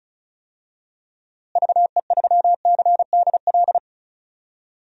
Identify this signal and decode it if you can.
Morse code, heard as a keyed tone, about 36 wpm, 700 Hz. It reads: VE3CDL
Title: DXCC-100 Call Signs - 6 Characters - Repeated 2x - 35wpm